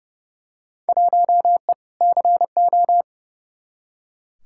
Morse code, heard as a keyed tone, about 30 wpm, 700 Hz. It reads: 1E CO
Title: Field Day Exchanges - 30wpm